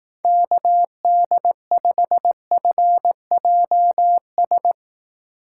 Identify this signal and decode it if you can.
Morse code, heard as a keyed tone, about 18 wpm, 700 Hz. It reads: KD5FJS
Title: Speed Racing - USA 2x3 Call Signs 15wpm